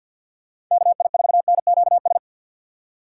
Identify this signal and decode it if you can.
Morse code, heard as a keyed tone, about 47 wpm, 700 Hz. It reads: KI4NQS
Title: USA Call Signs - 2x Character Spacing - 50wpm